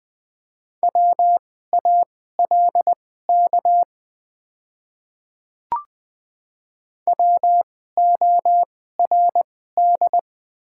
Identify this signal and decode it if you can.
Morse code, heard as a keyed tone, about 20 wpm, spaced as for 14 wpm, 700 Hz. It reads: WALK WORD
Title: 2x Character Spacing - Top 300 Words - 20wpm